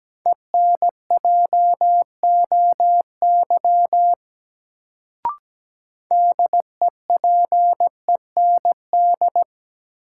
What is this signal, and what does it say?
Morse code, heard as a keyed tone, about 17 wpm, 700 Hz. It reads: ENJOY DEPEND
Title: Sentences from Top 1300 Words - New Word Drill Alt - 17wpm